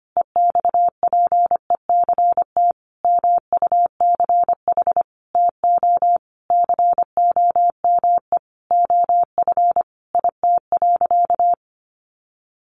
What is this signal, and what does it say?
Morse code, heard as a keyed tone, about 25 wpm, 700 Hz. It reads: EXPECT MUCH TO COME OF IT.
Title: Sentences from Top 1100 Words 25wpm